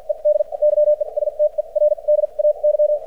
Morse code, heard as a keyed tone, about 54 wpm, spaced as for 40 wpm, 600 Hz. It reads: ED YESTERDAY
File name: archived / CRSnet